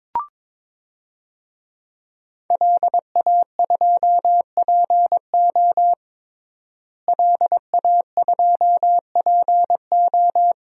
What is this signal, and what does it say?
Morse code, heard as a keyed tone, about 22 wpm, 700 Hz. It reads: LA2PO LA2PO
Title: DXCC-100 Call Signs - 5 Characters - Repeated 3x - 22wpm